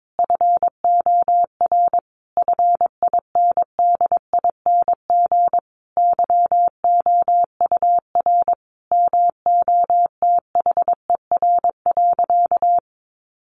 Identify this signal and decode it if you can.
Morse code, heard as a keyed tone, about 22 wpm, 700 Hz. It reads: FOR FINDING YOUR MOTHER.
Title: Sentences from Top 1300 Words 22wpm